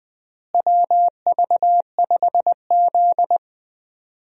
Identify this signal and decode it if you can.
Morse code, heard as a keyed tone, about 20 wpm, 700 Hz. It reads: WV5Z
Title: USA Call Signs - 4 Characters - Repeated 3x - 20wpm